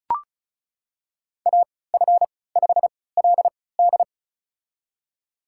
Morse code, heard as a keyed tone, about 36 wpm, spaced as for 20 wpm, 700 Hz. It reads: AF5LD
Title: USA Call Signs - 5 Characters - 3x Character Spacing - 35wpm